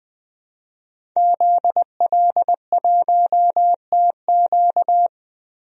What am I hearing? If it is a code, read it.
Morse code, heard as a keyed tone, about 20 wpm, 700 Hz. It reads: ZL1TQ
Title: DXCC-100 Call Signs - 5 Characters - 20wpm